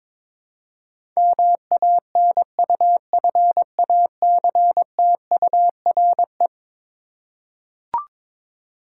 Morse code, heard as a keyed tone, about 22 wpm, 700 Hz. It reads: MANUFACTURE
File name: practice / 22wpm / Eleven Letter Words - 22wpm